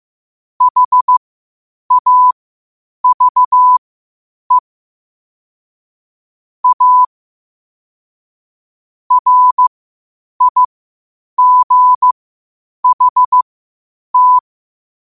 Morse code, heard as a keyed tone, about 15 wpm, spaced as for 8 wpm, 1000 Hz. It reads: HAVE A RIGHT
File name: qotdfiles / 8wpm